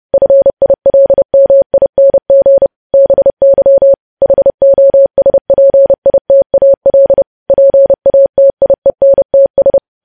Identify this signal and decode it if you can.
Morse code, heard as a keyed tone, about 30 wpm, 550 Hz. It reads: FILMING BY HOSPITAL PATIENTS
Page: news headlines in morse code at 30 wpm